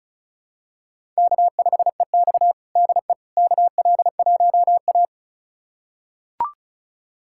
Morse code, heard as a keyed tone, about 34 wpm, 700 Hz. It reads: K5EX DE KL1A
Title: USA Callsign DE Callsign - 4 Characters - 35wpm